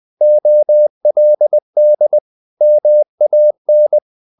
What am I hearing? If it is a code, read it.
Morse code, heard as a keyed tone, about 20 wpm, 600 Hz. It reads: OLD MAN
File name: the old man and the sea / the old man and the sea part 1 20wpm